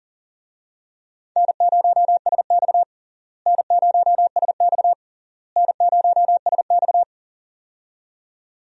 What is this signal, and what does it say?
Morse code, heard as a keyed tone, about 40 wpm, 700 Hz. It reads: N0SX N0SX N0SX
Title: USA Call Signs - 4 Characters - Repeated 3x - 40wpm